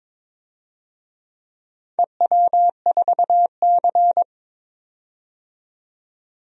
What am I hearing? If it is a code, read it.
Morse code, heard as a keyed tone, about 22 wpm, 700 Hz. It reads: EW4C
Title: DXCC-100 Call Signs - 4 Characters - 22wpm